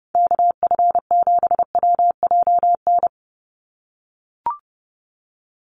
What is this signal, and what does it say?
Morse code, heard as a keyed tone, about 30 wpm, 700 Hz. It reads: KF7WJN
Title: USA Call Signs 30wpm